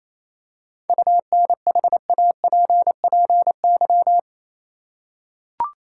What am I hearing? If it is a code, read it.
Morse code, heard as a keyed tone, about 27 wpm, 700 Hz. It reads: UNHAPPY
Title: Seven Letter Words - 28wpm